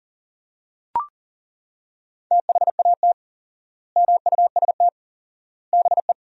Morse code, heard as a keyed tone, about 37 wpm, 700 Hz. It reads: THAT MUST BE
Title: 4x Word-Spacing - Sentences from Top 400 Words - 40wpm